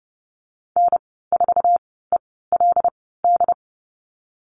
Morse code, heard as a keyed tone, about 30 wpm, spaced as for 17 wpm, 700 Hz. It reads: N4ELD
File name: Lesson 19g - Letter w - Call Signs 30wpm